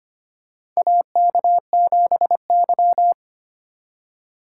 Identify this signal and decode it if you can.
Morse code, heard as a keyed tone, about 25 wpm, 700 Hz. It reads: AK7Y
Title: USA Call Signs - 4 Characters - 3x Character Spacing - 25wpm